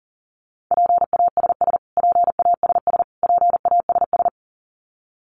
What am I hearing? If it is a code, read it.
Morse code, heard as a keyed tone, about 40 wpm, 700 Hz. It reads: PASS PASS PASS
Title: Top 500 Words Repeated 40wpm